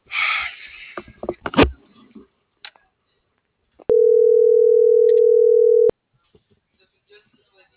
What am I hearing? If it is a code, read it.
Ringback tone (North America), 440 + 480 Hz.